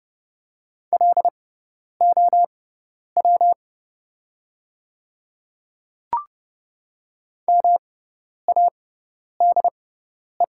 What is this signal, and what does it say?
Morse code, heard as a keyed tone, about 30 wpm, spaced as for 10 wpm, 700 Hz. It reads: LOW MADE
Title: 6x Character Spacing - Top 200 Words - 30wpm